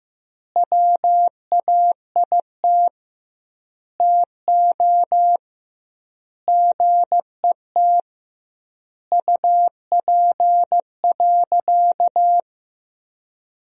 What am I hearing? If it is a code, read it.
Morse code, heard as a keyed tone, about 15 wpm, 700 Hz. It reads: WAIT TO GET UP.
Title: 2x Word-Spacing - Sentences from Top 500 Words - 15wpm